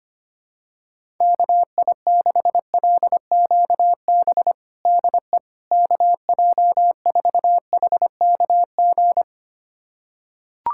Morse code, heard as a keyed tone, about 25 wpm, 700 Hz. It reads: KI6LQB DE KJ4HKG
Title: USA Callsign DE Callsign - 6 Characters - 25wpm